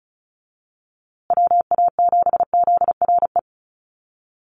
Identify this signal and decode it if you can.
Morse code, heard as a keyed tone, about 34 wpm, 700 Hz. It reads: WA7ZRE
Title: USA Call Signs Repeated 2x 35wpm